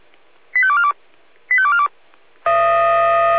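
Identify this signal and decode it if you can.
Five-tone selective call (ZVEI), heard as quick runs of five tones, tones of about 70 ms. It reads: 85242 85242
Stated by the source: ZVEI five-tone call received by SDR.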